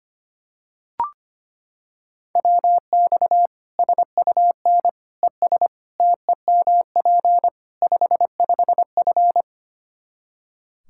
Morse code, heard as a keyed tone, about 25 wpm, 700 Hz. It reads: WX SUN ES TEMP 55F
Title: K1USN SST QSO Practice 25wpm